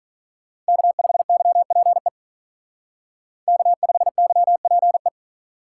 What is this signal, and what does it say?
Morse code, heard as a keyed tone, about 42 wpm, 700 Hz. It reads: K5YPE K5YPE
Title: Speed Racing - USA 1x3 Call Signs 35wpm